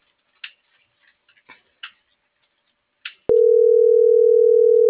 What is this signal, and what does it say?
Ringback tone (North America), 440 + 480 Hz.
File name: CENTRAL NMP / out melanie.diaznmp - .43333